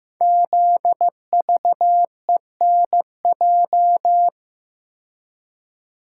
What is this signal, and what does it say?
Morse code, heard as a keyed tone, about 15 wpm, 700 Hz. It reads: ZVENJ
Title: Code Groups - 5 Letters - 15wpm